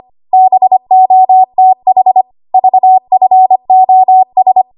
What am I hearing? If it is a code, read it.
Morse code, heard as a keyed tone, about 25 wpm, 750 Hz. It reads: BOTH VFOS
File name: Archive / 25 WPM / 25WPM